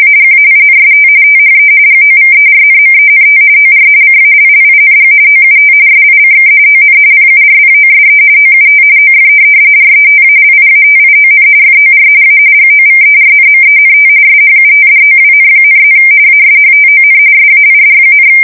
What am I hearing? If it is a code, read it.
RTTY, about 45.45 baud. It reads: AGAINST THE GOVERNMENT OR CALLING OUT SICK ON PURPOSE OUT OF RETALIATION FOR A GOVERNMENT SHUTDOWN,+ ASHTON SAID,
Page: 170 Shift 2125/2295